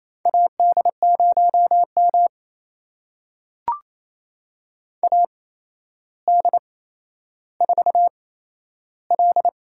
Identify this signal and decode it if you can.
Morse code, heard as a keyed tone, about 27 wpm, spaced as for 12 wpm, 700 Hz. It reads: AD0M AD4L
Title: USA Call Signs - 4 Characters - 8x Character Spacing - 28wpm